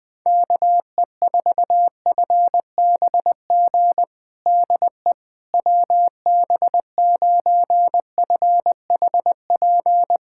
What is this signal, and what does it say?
Morse code, heard as a keyed tone, about 20 wpm, 700 Hz. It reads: KE4FBG DE WB9FHP
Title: USA Callsign DE Callsign - 6 Characters - 20wpm